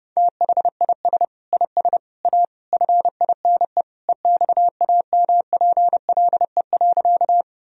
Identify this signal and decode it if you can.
Morse code, heard as a keyed tone, about 32 wpm, 700 Hz. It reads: THIS IS A FINE EXAMPLE.
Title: Sentences from Top 1000 Words - Review Part 1 - 30wpm